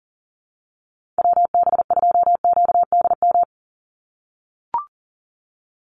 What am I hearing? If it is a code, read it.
Morse code, heard as a keyed tone, about 40 wpm, 690 Hz. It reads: WB2QDM
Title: USA Call Signs 40wpm